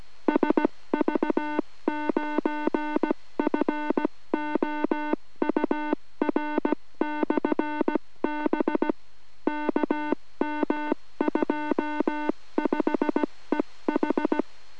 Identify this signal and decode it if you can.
Morse code, heard as a keyed tone, about 17 wpm, 3300 Hz, with noise. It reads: SV9FOUR/B KM25EH